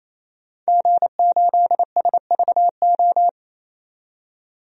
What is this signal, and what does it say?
Morse code, heard as a keyed tone, about 28 wpm, 700 Hz. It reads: G8SVO
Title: DXCC-100 Call Signs - 5 Characters - 28wpm